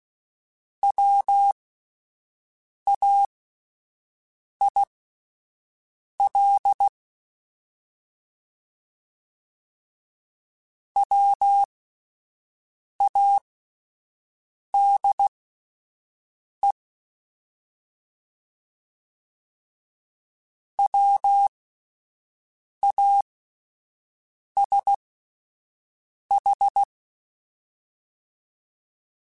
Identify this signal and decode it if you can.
Morse code, heard as a keyed tone, about 16 wpm, spaced as for 5 wpm, 800 Hz. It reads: WAIL WADE WASH
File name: Alphabetical / Alphabetical mp3s / Ws @5